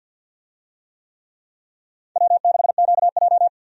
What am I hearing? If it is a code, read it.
Morse code, heard as a keyed tone, about 51 wpm, 700 Hz. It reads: W6QJ
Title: USA Call Signs - 4 Characters - 50wpm